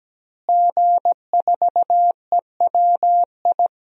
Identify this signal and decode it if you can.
Morse code, heard as a keyed tone, about 17 wpm, 700 Hz. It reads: G4EWI
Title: DXCC-100 Call Signs - 5 Characters - Repeated 2x - 17wpm